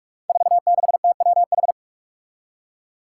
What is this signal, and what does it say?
Morse code, heard as a keyed tone, about 44 wpm, 700 Hz. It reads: 46TWH
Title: Five Challenging Letter-Number Combinations 45wpm